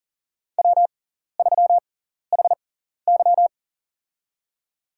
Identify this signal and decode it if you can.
Morse code, heard as a keyed tone, about 40 wpm, spaced as for 14 wpm, 690 Hz. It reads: W3HY
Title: USA Call Signs - 6x Character Spacing - 40wpm